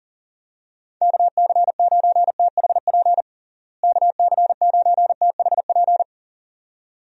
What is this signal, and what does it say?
Morse code, heard as a keyed tone, about 40 wpm, 700 Hz. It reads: KC9THP KC9THP
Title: USA Call Signs - 6 Characters - Repeated 2x - 40wpm